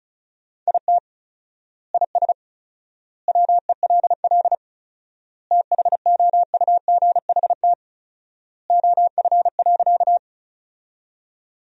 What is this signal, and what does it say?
Morse code, heard as a keyed tone, about 34 wpm, 700 Hz. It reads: IT IS WELL THOUGHT OF.
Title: 4x Word-Spacing - Sentences from Top 300 Words - 35wpm